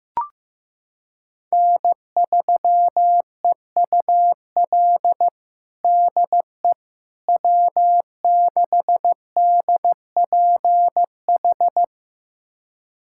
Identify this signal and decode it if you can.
Morse code, heard as a keyed tone, about 16 wpm, 700 Hz. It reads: N3EUL DE W6DPH
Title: USA Callsign DE Callsign - 5 Characters - 15wpm